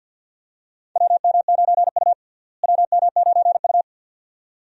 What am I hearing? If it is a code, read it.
Morse code, heard as a keyed tone, about 49 wpm, 700 Hz. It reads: WM9U WM9U
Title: USA Call Signs - 4 Characters - Repeated 2x - 50wpm